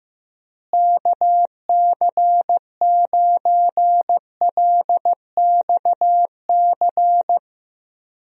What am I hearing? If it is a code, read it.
Morse code, heard as a keyed tone, about 15 wpm, 700 Hz. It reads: KC9LXC
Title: USA Call Signs - 6 Characters - Repeated 2x - 15wpm